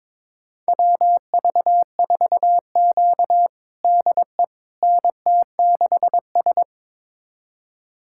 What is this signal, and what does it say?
Morse code, heard as a keyed tone, about 22 wpm, 700 Hz. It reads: WV4Q DE NT6S
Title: USA Callsign DE Callsign - 4 Characters - 22wpm